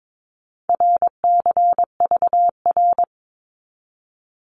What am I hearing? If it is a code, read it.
Morse code, heard as a keyed tone, about 22 wpm, 700 Hz. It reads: RCVR
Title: Abbreviations 22wpm